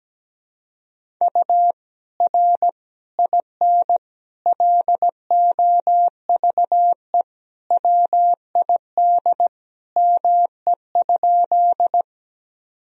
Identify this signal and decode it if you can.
Morse code, heard as a keyed tone, about 17 wpm, 700 Hz. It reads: U R IN LOVE WID ME?
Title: Sentences from Top 400 Words and Abbreviations 17wpm